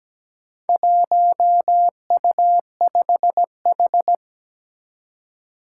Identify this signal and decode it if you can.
Morse code, heard as a keyed tone, about 17 wpm, 700 Hz. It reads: 1U5H
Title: Four Challenging Letter-Number Combinations 17wpm